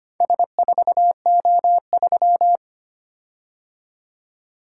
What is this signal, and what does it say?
Morse code, heard as a keyed tone, about 25 wpm, 700 Hz. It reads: S4O3
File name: practice / 25wpm / Four Letter-Number Combinations - 25wpm